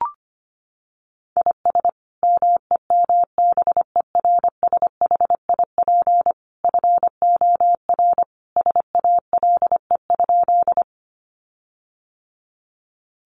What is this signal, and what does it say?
Morse code, heard as a keyed tone, about 25 wpm, 700 Hz. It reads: IS MEMBERSHIP FOR SALE?
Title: Sentences from Top 1400 Words 25wpm